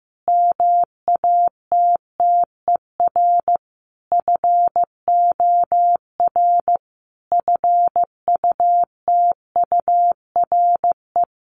Morse code, heard as a keyed tone, about 15 wpm, 700 Hz. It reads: MATTER FOR FUTURE
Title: Sentences from Top 1500 Words 15wpm